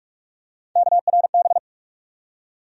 Morse code, heard as a keyed tone, about 47 wpm, 700 Hz. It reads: KRB
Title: USA Call Sign Postfixes - 3 Letters 45wpm